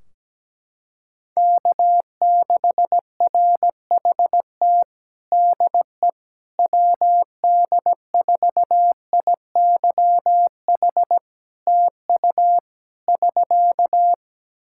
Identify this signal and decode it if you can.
Morse code, heard as a keyed tone, about 17 wpm, 700 Hz. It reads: K6RHT DE WD4IYH TU <SK>
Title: Practice QSO - Exchange 4 - Ending - 17wpm